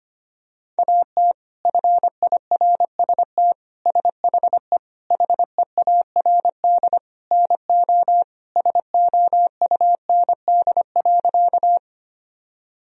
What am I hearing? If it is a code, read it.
Morse code, heard as a keyed tone, about 25 wpm, 700 Hz. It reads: AT FIRST SHE HEARD NO SOUND.
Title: Sentences from Top 1000 Words - Review Part 1 - 25wpm